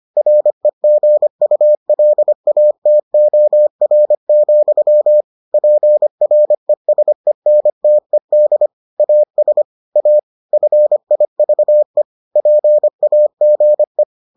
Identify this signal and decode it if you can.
Morse code, heard as a keyed tone, about 25 wpm, 600 Hz. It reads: REGULATOR, PRESENTED AS A FIVE PAGE